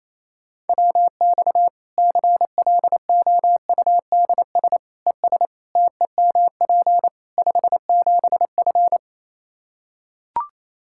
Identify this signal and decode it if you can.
Morse code, heard as a keyed tone, about 28 wpm, 700 Hz. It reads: WX CLOUDS ES TEMP 57F
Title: Practice QSO - Exchange 2 - Info - 28wpm